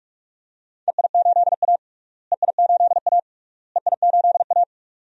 Morse code, heard as a keyed tone, about 43 wpm, 700 Hz. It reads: EI8A EI8A EI8A
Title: DXCC-100 Call Signs - 4 Characters - Repeated 3x - 45wpm